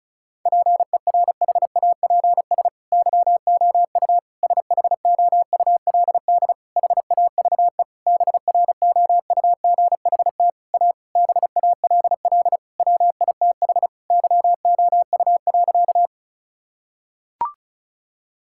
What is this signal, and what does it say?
Morse code, heard as a keyed tone, about 34 wpm, 700 Hz. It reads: PERHAPS YOU SHOULD HAVE BROUGHT A BALL WITH YOU.
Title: Sentences from Top 1000 Words - Review Part 6 - 35wpm